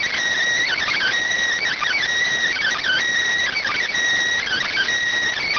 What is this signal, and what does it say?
Emergency Alert System (SAME) tones (header bursts), heard as harsh buzzing bursts. It is heard from 0.6 to 4.9 s.
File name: findings / CT 205 ax mips, iDose 3 0006.gen-42 rx